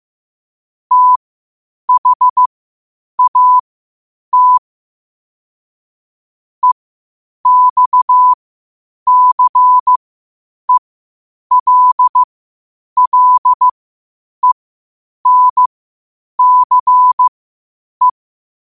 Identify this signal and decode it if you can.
Morse code, heard as a keyed tone, about 15 wpm, spaced as for 8 wpm, 1000 Hz. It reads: THAT EXCELLENCE
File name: qotdfiles / 8wpm